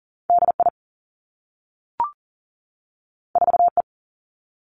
Morse code, heard as a keyed tone, about 38 wpm, 700 Hz. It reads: DI 4E
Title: Two Challenging Letter-Number Combinations 40wpm